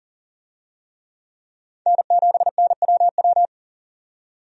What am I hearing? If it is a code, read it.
Morse code, heard as a keyed tone, about 40 wpm, 700 Hz. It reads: N7NWW
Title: USA Call Signs - 5 Characters - 40wpm